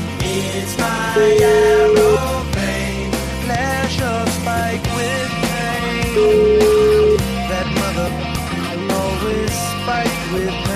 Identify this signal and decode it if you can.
Ringback tone (Europe, 425 Hz), with noise.